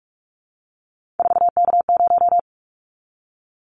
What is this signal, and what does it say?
Morse code, heard as a keyed tone, about 46 wpm, 700 Hz. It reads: 4K0
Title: Callsign Prefixes for Top 100 DXCC - 45wpm